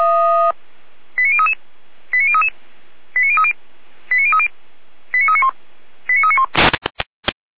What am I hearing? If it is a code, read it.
Five-tone selective call (ZVEI), heard as quick runs of five tones, tones of about 70 ms. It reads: 89033 89033 89030 89030 89391 89391